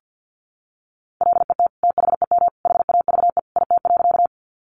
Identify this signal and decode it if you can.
Morse code, heard as a keyed tone, about 50 wpm, 700 Hz. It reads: LET THEM HAVE IT.
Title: Sentences from Top 300 Words 50wpm